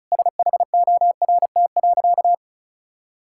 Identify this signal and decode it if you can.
Morse code, heard as a keyed tone, about 36 wpm, 700 Hz. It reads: SHORT.
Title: Sentences from Top 1600 Words 35wpm